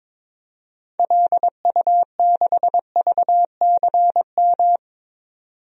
Morse code, heard as a keyed tone, about 21 wpm, 700 Hz. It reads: LU6VCM
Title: DXCC-100 Call Signs - 6 Characters - 22wpm